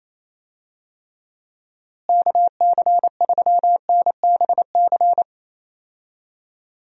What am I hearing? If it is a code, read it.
Morse code, heard as a keyed tone, about 27 wpm, 700 Hz. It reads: KC3NBC
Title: USA Call Signs - 6 Characters - Repeated 2x - 28wpm